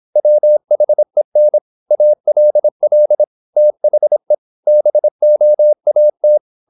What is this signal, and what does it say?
Morse code, heard as a keyed tone, about 26 wpm, 600 Hz. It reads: WHEN ALL THE BOAT
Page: Never 26wpm | CW med Gnister